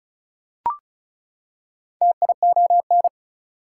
Morse code, heard as a keyed tone, about 36 wpm, 700 Hz. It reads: TION
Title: Prefixes and Suffixes 35wpm